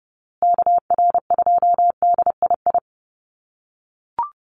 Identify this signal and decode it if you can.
Morse code, heard as a keyed tone, about 30 wpm, 700 Hz. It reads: KR2DII
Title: USA Call Signs Repeated 2x 30wpm